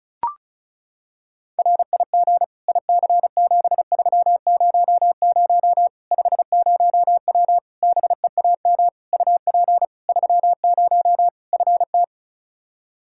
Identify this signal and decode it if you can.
Morse code, heard as a keyed tone, about 36 wpm, 700 Hz. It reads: RIG IC7300 50W BEAM UP 30 FT
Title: Practice QSO - Exchange 2 - Info - 35wpm